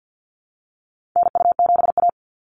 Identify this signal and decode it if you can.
Morse code, heard as a keyed tone, about 50 wpm, 700 Hz. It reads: NU7A
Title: NA SOTA Activator Callsigns 50wpm